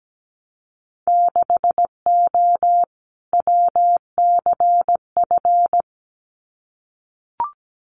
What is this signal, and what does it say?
Morse code, heard as a keyed tone, about 17 wpm, 700 Hz. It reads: 6O WCF
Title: Winter Field Day Exchanges Encore 17wpm